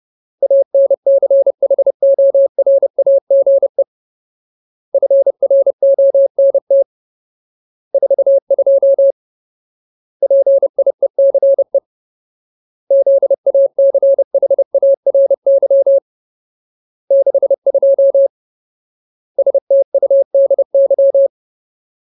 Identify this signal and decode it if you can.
Morse code, heard as a keyed tone, about 30 wpm, 550 Hz. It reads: ANCHORAGE FRONT 42 PIECE ZACHARY 62 STUDY
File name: Ultimate Word List / 30WPM Ultimate